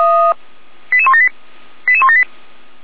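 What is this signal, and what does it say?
Five-tone selective call (ZVEI), heard as quick runs of five tones, tones of about 70 ms. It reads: 88178 88178
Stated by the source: ZVEI five-tone selective call received by SDR.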